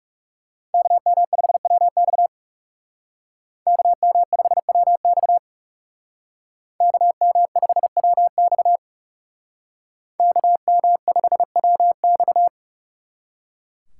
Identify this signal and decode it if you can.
Morse code, heard as a keyed tone, about 37 wpm, 700 Hz. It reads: KM5WX KM5WX KM5WX KM5WX
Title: Speed Racing - USA 2x2 Call Signs 30wpm